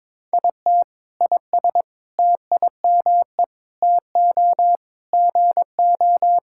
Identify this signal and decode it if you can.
Morse code, heard as a keyed tone, about 22 wpm, 700 Hz. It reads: IT IS TIME TO GO
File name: Sentences with 7-Word Phrases - 22wpm